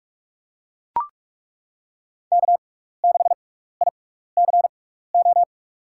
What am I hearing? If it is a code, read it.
Morse code, heard as a keyed tone, about 42 wpm, spaced as for 15 wpm, 700 Hz. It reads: K6ICO
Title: USA Call Signs - 5 Characters - 6x Character Spacing - 45wpm